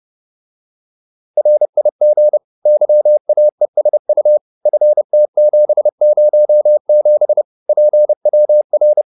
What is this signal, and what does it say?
Morse code, heard as a keyed tone, about 30 wpm, 600 Hz. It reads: RIG YAESU FT707 PWR